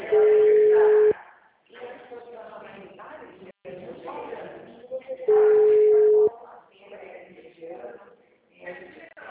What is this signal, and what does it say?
Ringback tone (Europe, 425 Hz), with speech.
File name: out- -2016- - - .95141